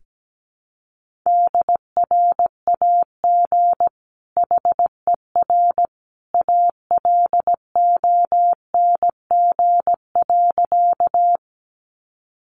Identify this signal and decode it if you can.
Morse code, heard as a keyed tone, about 17 wpm, 700 Hz. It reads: DRAG HER ALONG.
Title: Sentences from Top 1500 Words 17wpm